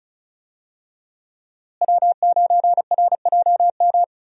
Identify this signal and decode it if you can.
Morse code, heard as a keyed tone, about 34 wpm, 700 Hz. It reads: W9RJM
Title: USA Call Signs - 5 Characters - Repeated 2x - 35wpm